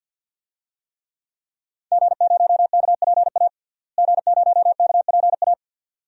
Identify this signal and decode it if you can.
Morse code, heard as a keyed tone, about 47 wpm, 700 Hz. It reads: G0KPA G0KPA
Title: DXCC-100 Call Signs - 5 Characters - Repeated 2x - 50wpm